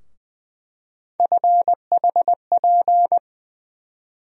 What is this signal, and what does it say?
Morse code, heard as a keyed tone, about 21 wpm, 700 Hz. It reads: FHP
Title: USA Call Sign Postfixes - 3 Letters 20wpm